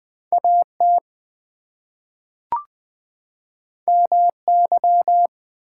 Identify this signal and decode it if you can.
Morse code, heard as a keyed tone, about 20 wpm, 700 Hz. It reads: AT MY
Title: Two Letter Words - 20wpm